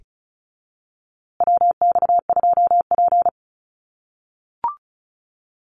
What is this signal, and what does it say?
Morse code, heard as a keyed tone, about 34 wpm, 700 Hz. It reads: WX2P
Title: USA Call Signs Repeated 3x 35wpm